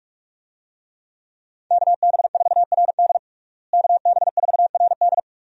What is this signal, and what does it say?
Morse code, heard as a keyed tone, about 46 wpm, 700 Hz. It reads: KB4RD KB4RD
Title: USA Call Signs - 5 Characters - Repeated 2x - 45wpm